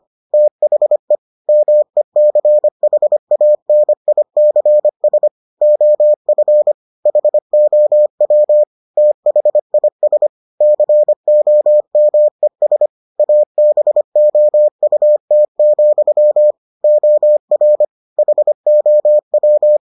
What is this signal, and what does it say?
Morse code, heard as a keyed tone, about 25 wpm, 600 Hz. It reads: THE MECHANICS OF HOW THIS COMES ABOUT, OR HOW